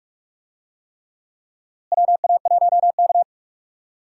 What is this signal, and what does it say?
Morse code, heard as a keyed tone, about 46 wpm, 700 Hz. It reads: WA1K
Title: USA Call Signs - 4 Characters - Repeated 3x - 45wpm